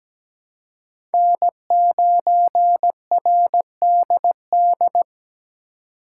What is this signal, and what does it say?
Morse code, heard as a keyed tone, about 17 wpm, 700 Hz. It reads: N9RDD
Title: USA Call Signs - 5 Characters - 2x Character Spacing - 17wpm